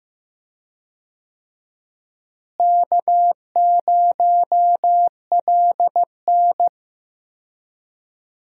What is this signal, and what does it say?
Morse code, heard as a keyed tone, about 15 wpm, 700 Hz. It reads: K0LN
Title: USA Call Signs - 4 Characters - Repeated 3x - 15wpm